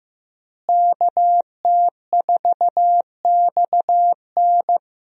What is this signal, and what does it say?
Morse code, heard as a keyed tone, about 15 wpm, 700 Hz. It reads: KT4XN
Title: USA Call Signs - 5 Characters - Repeated 3x - 15wpm